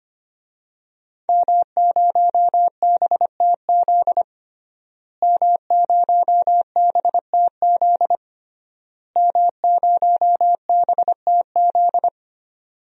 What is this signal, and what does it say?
Morse code, heard as a keyed tone, about 25 wpm, 700 Hz. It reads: M0BTZ M0BTZ M0BTZ
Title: DXCC-100 Call Signs - 5 Characters - Repeated 3x - 25wpm